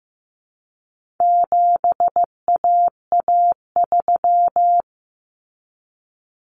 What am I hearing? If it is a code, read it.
Morse code, heard as a keyed tone, about 15 wpm, 700 Hz. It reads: 7AA3
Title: Common Cut Numbers - 4-Digit 15wpm